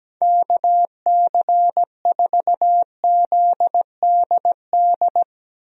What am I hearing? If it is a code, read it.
Morse code, heard as a keyed tone, about 17 wpm, 700 Hz. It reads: KC4ZDD
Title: USA Call Signs - 6 Characters - 17wpm